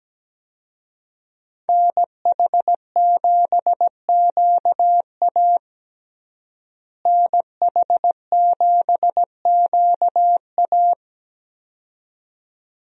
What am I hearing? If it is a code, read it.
Morse code, heard as a keyed tone, about 17 wpm, 700 Hz. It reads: NH7QA NH7QA
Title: USA Call Signs - 5 Characters - Repeated 2x - 17wpm